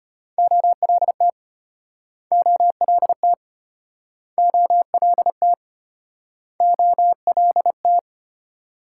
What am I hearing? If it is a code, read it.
Morse code, heard as a keyed tone, about 33 wpm, 700 Hz. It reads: OLT OLT OLT OLT